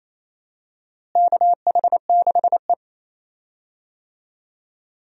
Morse code, heard as a keyed tone, about 28 wpm, 700 Hz. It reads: KH6E
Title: DXCC-100 Call Signs - 4 Characters - Repeated 2x - 28wpm